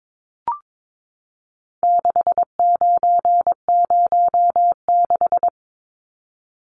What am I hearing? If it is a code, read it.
Morse code, heard as a keyed tone, about 22 wpm, 700 Hz. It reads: 6906
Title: Numbers - 4-Digit 22wpm